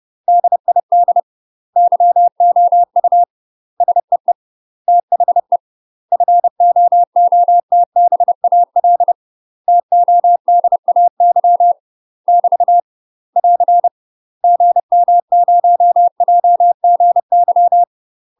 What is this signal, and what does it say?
Morse code, heard as a keyed tone, about 30 wpm, 700 Hz. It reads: DID YOU SEE THE FOOTBAL TODAY <BT> <AR> GM0JGY
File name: UK MASTER QSOs TRAINING / 30WPM / UK MASTER QSOs 0014